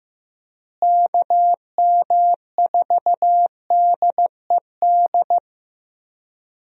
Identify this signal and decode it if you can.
Morse code, heard as a keyed tone, about 15 wpm, 700 Hz. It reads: KM4DED
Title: USA Call Signs - 6 Characters - Repeated 3x - 15wpm